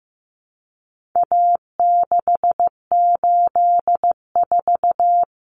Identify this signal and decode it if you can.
Morse code, heard as a keyed tone, about 15 wpm, 700 Hz. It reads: A684
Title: Common Cut Numbers - 4-Digit 15wpm